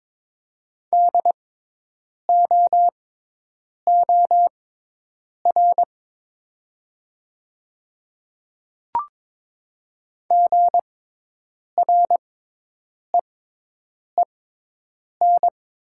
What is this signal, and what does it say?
Morse code, heard as a keyed tone, about 22 wpm, spaced as for 8 wpm, 700 Hz. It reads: DOOR GREEN
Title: 6x Character Spacing - Top 500 Words - 22wpm